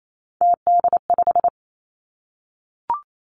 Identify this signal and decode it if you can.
Morse code, heard as a keyed tone, about 28 wpm, 700 Hz. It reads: TD5
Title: Three Challenging Letter-Number Combinations 28wpm